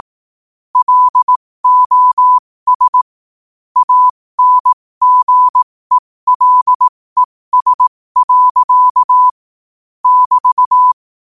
Morse code, heard as a keyed tone, about 18 wpm, spaced as for 15 wpm, 1000 Hz. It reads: LOS ANGELES. <BT>
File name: qotdfiles / 15wpm